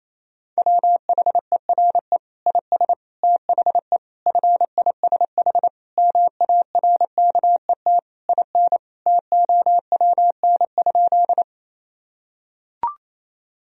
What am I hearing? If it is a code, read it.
Morse code, heard as a keyed tone, about 28 wpm, 700 Hz. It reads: WHERE IS THE FISH MARKET IN TOWN?
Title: Sentences from Top 1000 Words - Review Part 3 - 28wpm